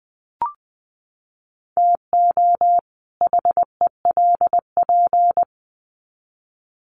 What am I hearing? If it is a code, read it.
Morse code, heard as a keyed tone, about 20 wpm, 700 Hz. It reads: TO HELP
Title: Sets of 2 Words 20wpm